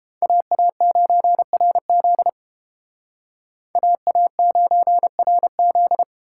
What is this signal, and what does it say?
Morse code, heard as a keyed tone, about 30 wpm, 700 Hz. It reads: AA9RZ AA9RZ